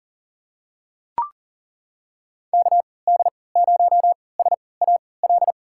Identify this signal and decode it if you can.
Morse code, heard as a keyed tone, about 38 wpm, spaced as for 22 wpm, 700 Hz. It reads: KD0SAL
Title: USA Call Signs - 6 Characters - 3x Character Spacing - 40wpm